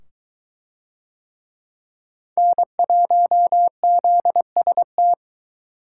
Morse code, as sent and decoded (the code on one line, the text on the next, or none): -. .---- --.. ... -
N1ZST